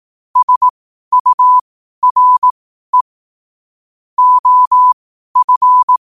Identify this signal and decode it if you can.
Morse code, heard as a keyed tone, about 18 wpm, spaced as for 12 wpm, 1000 Hz. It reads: SURE OF
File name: qotdfiles / 12wpm